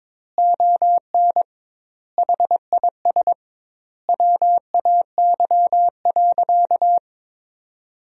Morse code, heard as a keyed tone, about 22 wpm, 700 Hz. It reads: ON HIS WAY.
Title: 2x Word-Spacing - Sentences from Top 200 Words - 22wpm